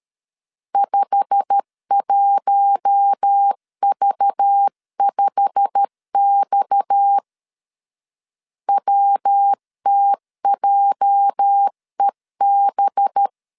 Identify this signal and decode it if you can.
Morse code, heard as a keyed tone, about 13 wpm, 800 Hz. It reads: 51V5X WTJEB
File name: LEZIONI / 8 YJ45 / 8 YJ45 12wpm 2151